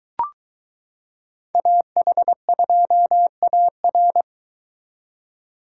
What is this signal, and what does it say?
Morse code, heard as a keyed tone, about 22 wpm, 700 Hz. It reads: AH2AR